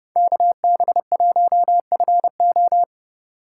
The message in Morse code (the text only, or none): KB1FO